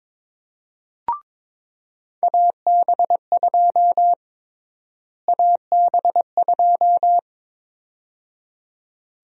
Morse code, heard as a keyed tone, about 22 wpm, 700 Hz. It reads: AB2 AB2
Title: USA Call Sign Prefixes - Repeated 2x - 22wpm